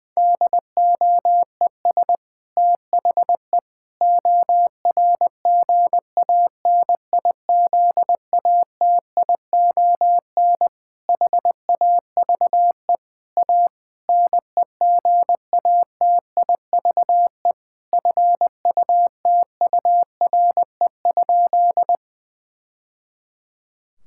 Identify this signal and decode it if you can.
Morse code, heard as a keyed tone, about 21 wpm, 700 Hz. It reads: DOES THE ORGANIZATION HAVE A NEGATIVE FUTURE?
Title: Sentences from Top 1000 Words - Review Part 6 - 20wpm